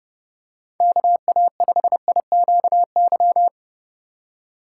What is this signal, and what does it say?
Morse code, heard as a keyed tone, about 30 wpm, 700 Hz. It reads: KA5IQY
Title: USA Call Signs - 6 Characters - 3x Character Spacing - 30wpm